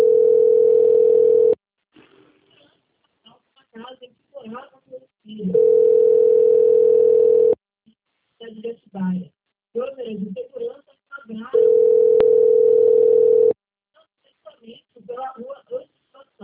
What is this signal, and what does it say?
Ringback tone (North America), 440 + 480 Hz, with speech.
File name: rg-1600-unknown- - - .39229